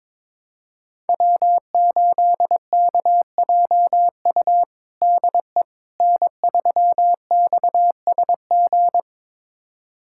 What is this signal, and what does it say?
Morse code, heard as a keyed tone, about 21 wpm, 700 Hz. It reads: W8KJU DE N3XSG
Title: USA Callsign DE Callsign - 5 Characters - 22wpm